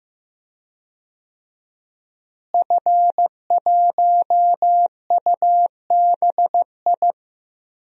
Morse code, heard as a keyed tone, about 15 wpm, 700 Hz. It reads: F1UBI
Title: DXCC-100 Call Signs - 5 Characters - Repeated 3x - 15wpm